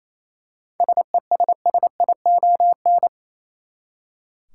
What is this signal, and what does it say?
Morse code, heard as a keyed tone, about 28 wpm, 700 Hz. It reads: SESSION
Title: Seven Letter Words - 28wpm